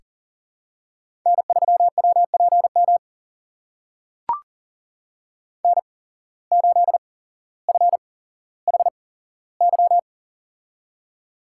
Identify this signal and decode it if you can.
Morse code, heard as a keyed tone, about 40 wpm, spaced as for 17 wpm, 700 Hz. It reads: N3WPM N8FHY
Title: USA Call Signs - 5 Characters - 8x Character Spacing - 40wpm